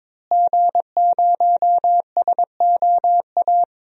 Morse code, heard as a keyed tone, about 22 wpm, 700 Hz. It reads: G0SOA
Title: DXCC-100 Call Signs - 5 Characters - Repeated 3x - 22wpm